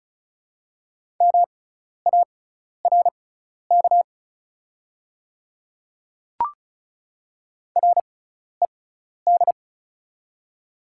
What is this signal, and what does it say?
Morse code, heard as a keyed tone, about 33 wpm, spaced as for 12 wpm, 690 Hz. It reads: MARK RED
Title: 6x Character Spacing - Top 500 Words - 35wpm